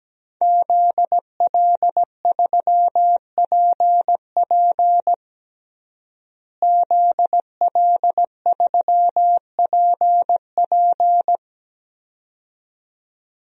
Morse code, heard as a keyed tone, about 17 wpm, 700 Hz. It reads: ZL3PP ZL3PP
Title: DXCC-100 Call Signs - 5 Characters - Repeated 2x - 17wpm